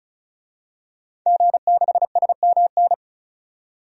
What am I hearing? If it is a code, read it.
Morse code, heard as a keyed tone, about 34 wpm, 700 Hz. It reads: G6SMN
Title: DXCC-100 Call Signs - 5 Characters - 35wpm